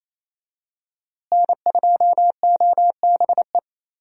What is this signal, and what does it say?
Morse code, heard as a keyed tone, about 28 wpm, 700 Hz. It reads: N2OBE
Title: USA Call Signs - 5 Characters - Repeated 3x - 28wpm